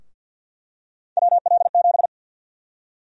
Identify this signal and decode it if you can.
Morse code, heard as a keyed tone, about 49 wpm, 700 Hz. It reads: WL7
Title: USA Call Sign Prefixes - 50wpm